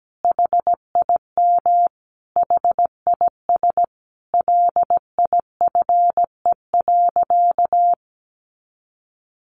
Morse code, heard as a keyed tone, about 17 wpm, 700 Hz. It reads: HIM HIS LIFE.
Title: Sentences from Top 1500 Words 17wpm